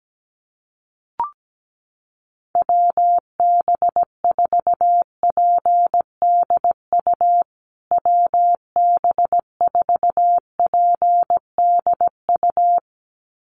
Morse code, heard as a keyed tone, about 17 wpm, 700 Hz. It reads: WB4PDU WB4PDU
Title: USA Call Signs Repeated 2x 17wpm